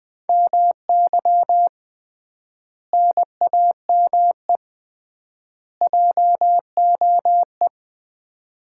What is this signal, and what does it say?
Morse code, heard as a keyed tone, about 20 wpm, 700 Hz. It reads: MY NAME JOE
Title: Common QSO Phrases - 3x Word-Spacing - 20wpm